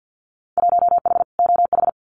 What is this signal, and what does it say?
Morse code, heard as a keyed tone, about 50 wpm, 700 Hz. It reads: 1H OH
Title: Winter Field Day Exchanges 50wpm